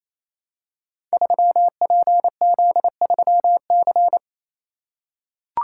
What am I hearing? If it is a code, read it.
Morse code, heard as a keyed tone, about 27 wpm, 700 Hz. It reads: 3PZ3C
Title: Code Groups - 5 Letter:Numbers - 28wpm